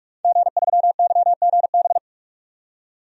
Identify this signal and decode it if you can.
Morse code, heard as a keyed tone, about 46 wpm, 700 Hz. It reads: G3YGB
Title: DXCC-100 Call Signs - 5 Characters - 45wpm